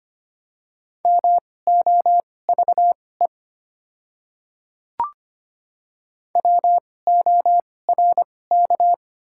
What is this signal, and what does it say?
Morse code, heard as a keyed tone, about 25 wpm, spaced as for 18 wpm, 690 Hz. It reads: MOVE WORK
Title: 2x Character Spacing - Top 200 Words - 25wpm